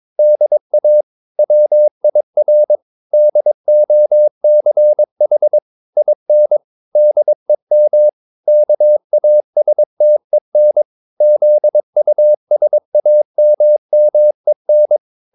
Morse code, heard as a keyed tone, about 22 wpm, 600 Hz. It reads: DA WIR DOCH IN DEM KASTEN ZUSAMMEN